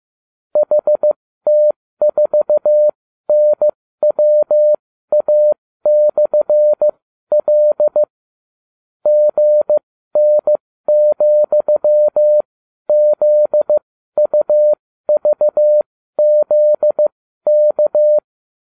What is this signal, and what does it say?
Morse code, heard as a keyed tone, about 15 wpm, spaced as for 12 wpm, 600 Hz. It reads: HT4NWA/L GN,ZUVZK